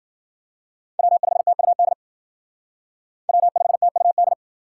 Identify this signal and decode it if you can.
Morse code, heard as a keyed tone, about 50 wpm, 700 Hz. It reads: W5TUD W5TUD
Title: Speed Racing - USA 1x3 Call Signs 40wpm